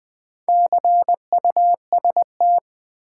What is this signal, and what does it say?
Morse code, heard as a keyed tone, about 20 wpm, 700 Hz. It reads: CUST
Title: Build A Word - 2 Letters At A Time - 8-Letter Words - 20wpm